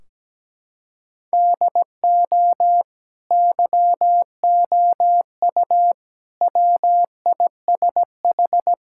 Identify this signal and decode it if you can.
Morse code, heard as a keyed tone, about 17 wpm, 700 Hz. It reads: DO YOU WISH
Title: Sentences from Top 1600 Words 17wpm